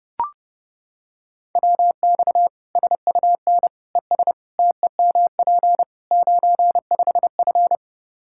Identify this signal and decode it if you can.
Morse code, heard as a keyed tone, about 32 wpm, 700 Hz. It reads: WX SUN ES TEMP 95F
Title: K1USN SST QSO Practice 30wpm